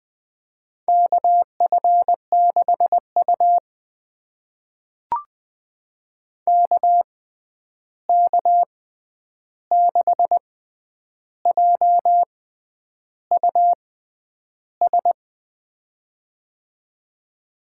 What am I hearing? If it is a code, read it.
Morse code, heard as a keyed tone, about 20 wpm, spaced as for 9 wpm, 700 Hz. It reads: KF6U KK6JUS
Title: USA Call Signs - 6x Character Spacing - 20wpm